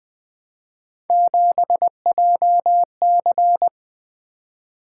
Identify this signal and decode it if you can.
Morse code, heard as a keyed tone, about 20 wpm, 700 Hz. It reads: 7JC
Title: Three Letter-Number Combinations - 20wpm